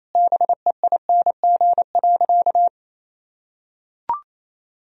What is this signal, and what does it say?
Morse code, heard as a keyed tone, about 28 wpm, 700 Hz. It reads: BEING.
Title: Sentences from Top 1000 Words - Review Part 4 - 28wpm